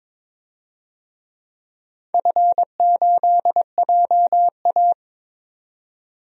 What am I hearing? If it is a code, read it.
Morse code, heard as a keyed tone, about 22 wpm, 700 Hz. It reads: F8JA
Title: DXCC-100 Call Signs - 4 Characters - 22wpm